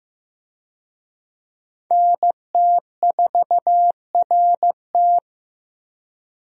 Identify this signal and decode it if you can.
Morse code, heard as a keyed tone, about 15 wpm, 700 Hz. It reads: NT4RT
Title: USA Call Signs - 5 Characters - Repeated 3x - 15wpm